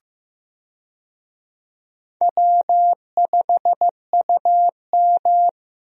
Morse code, heard as a keyed tone, about 16 wpm, 700 Hz. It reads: W5UM
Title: USA Call Signs - 4 Characters - Repeated 2x - 15wpm